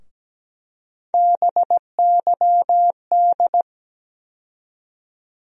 Morse code, heard as a keyed tone, about 17 wpm, 700 Hz. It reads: BYD
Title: USA Call Sign Postfixes - 3 Letters 17wpm